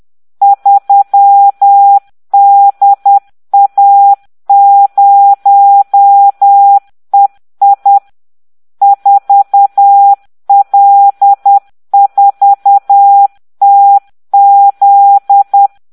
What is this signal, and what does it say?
Morse code, heard as a keyed tone, about 10 wpm, 800 Hz. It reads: 3DA0EI 4L4TZ